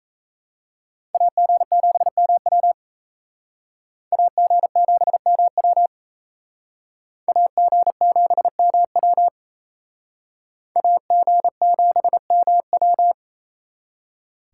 Morse code, heard as a keyed tone, about 35 wpm, 700 Hz. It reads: AG7MW AG7MW AG7MW AG7MW
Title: Speed Racing - USA 2x2 Call Signs 28wpm